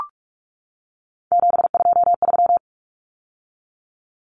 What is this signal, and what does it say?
Morse code, heard as a keyed tone, about 44 wpm, 700 Hz. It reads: 723
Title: Common Cut Numbers - 3-Digit 45wpm